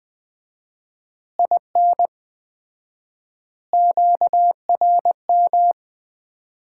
Morse code, heard as a keyed tone, about 21 wpm, 700 Hz. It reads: IN QRM
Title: Common QSO Phrases - 4x Word-Spacing - 20wpm